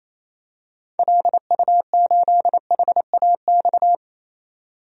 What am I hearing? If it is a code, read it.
Morse code, heard as a keyed tone, about 28 wpm, 700 Hz. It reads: LU8HAX
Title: DXCC-100 Call Signs - 6 Characters - 28wpm